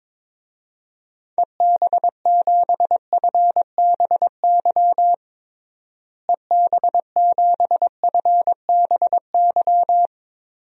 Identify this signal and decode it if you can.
Morse code, heard as a keyed tone, about 22 wpm, 700 Hz. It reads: EB7FBY EB7FBY
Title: DXCC-100 Call Signs - 6 Characters - Repeated 3x - 22wpm